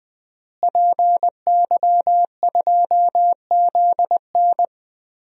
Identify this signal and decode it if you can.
Morse code, heard as a keyed tone, about 20 wpm, 700 Hz. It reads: PY2ZN
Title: DXCC-100 Call Signs - 5 Characters - Repeated 3x - 20wpm